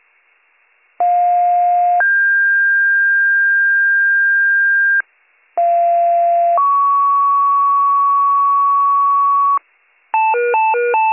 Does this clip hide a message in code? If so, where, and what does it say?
Yes: two-tone page from 1.0 to 9.6 s, heard as a one-second tone, then a three-second tone, about 710 then 1600 Hz, about 690 then 1120 Hz, with noise.